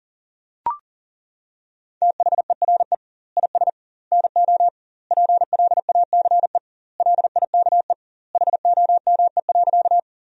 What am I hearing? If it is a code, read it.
Morse code, heard as a keyed tone, about 40 wpm, 700 Hz. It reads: THERE IS NO PLACE LIKE HOME.
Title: 2x Word-Spacing - Sentences from Top 200 Words - 40wpm